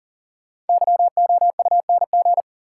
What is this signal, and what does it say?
Morse code, heard as a keyed tone, about 40 wpm, 700 Hz. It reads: YOUNG
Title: Sentences from Top 400 Words - New Word Drill - 40wpm